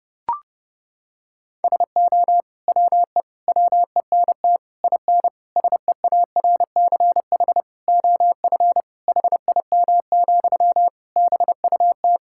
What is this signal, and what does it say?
Morse code, heard as a keyed tone, about 30 wpm, 700 Hz. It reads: SO WE WENT IN SEARCH OF HIM, BUT
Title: Sentences from Top 1000 Words - Review Part 5 - 30wpm